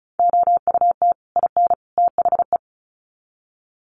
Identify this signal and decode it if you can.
Morse code, heard as a keyed tone, about 34 wpm, 700 Hz. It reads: OUT IN THE
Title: Sets of 3 Words 35wpm